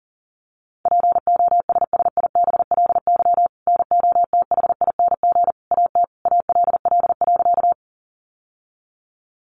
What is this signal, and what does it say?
Morse code, heard as a keyed tone, about 40 wpm, 700 Hz. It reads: POSSIBLY NOTHING AT ALL.
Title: Sentences from Top 1300 Words 40wpm